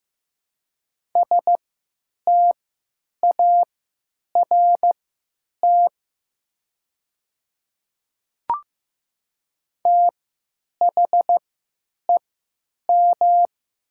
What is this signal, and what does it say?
Morse code, heard as a keyed tone, about 15 wpm, spaced as for 9 wpm, 700 Hz. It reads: START THEM
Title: 3x Character Spacing - Top 300 Words - 15wpm